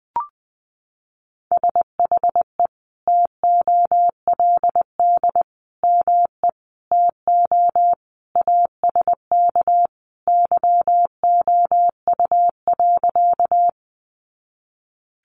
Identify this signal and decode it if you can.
Morse code, heard as a keyed tone, about 20 wpm, 700 Hz. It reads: SHE TOLD ME TO ASK YOU.
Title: Sentences from Top 400 Words 20wpm